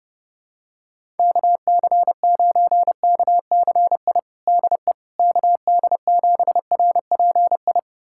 Morse code, heard as a keyed tone, about 30 wpm, 700 Hz. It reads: KC9KCI DE KD7RPI
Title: USA Callsign DE Callsign 30wpm